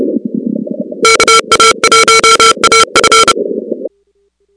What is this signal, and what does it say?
Morse code, heard as a keyed tone, about 29 wpm, 1350 Hz, with noise. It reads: KA1AF
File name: contests / 2019-cw-wpx-kt5j-k5pi- -2330 0-6